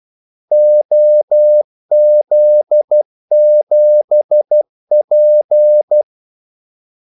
Morse code, heard as a keyed tone, about 12 wpm, 600 Hz. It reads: OZ7P
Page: 2 ekstra ordmellemrum mellem kaldesignalerne